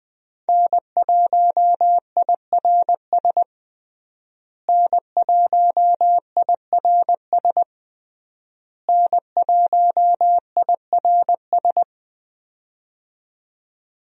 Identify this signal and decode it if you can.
Morse code, heard as a keyed tone, about 20 wpm, 700 Hz. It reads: N1IRS N1IRS N1IRS
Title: USA Call Signs - 5 Characters - Repeated 3x - 20wpm